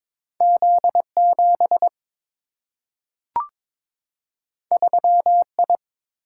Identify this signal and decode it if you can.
Morse code, heard as a keyed tone, about 22 wpm, 700 Hz. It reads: Z7 3I
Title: Single-Letter Single-Number 22wpm